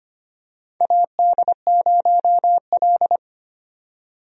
Morse code, heard as a keyed tone, about 25 wpm, 700 Hz. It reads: AD0L
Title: USA Call Signs - 4 Characters - 25wpm